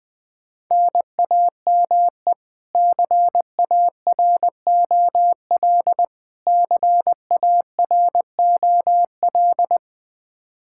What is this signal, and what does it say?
Morse code, heard as a keyed tone, about 20 wpm, 700 Hz. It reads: NAME CAROL CAROL
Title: Practice QSO - Exchange 1 - Beginning - 20wpm